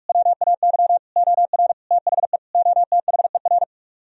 Morse code, heard as a keyed tone, about 45 wpm, 700 Hz. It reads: WAY OR THE OTHER